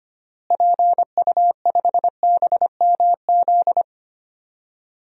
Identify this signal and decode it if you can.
Morse code, heard as a keyed tone, about 25 wpm, 700 Hz. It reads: PU5BMZ
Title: DXCC-100 Call Signs - 6 Characters - Repeated 3x - 25wpm